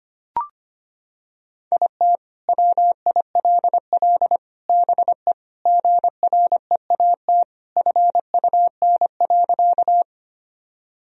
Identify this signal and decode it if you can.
Morse code, heard as a keyed tone, about 25 wpm, 700 Hz. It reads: IT WILL BE GREAT FUN.
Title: Sentences from Top 1000 Words - Review Part 4 - 25wpm